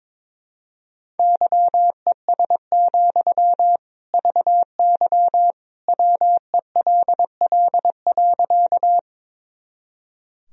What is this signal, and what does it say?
Morse code, heard as a keyed tone, about 22 wpm, 700 Hz. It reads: YES, VY WELL.
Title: Sentences from Top 500 Words and Abbreviations 22wpm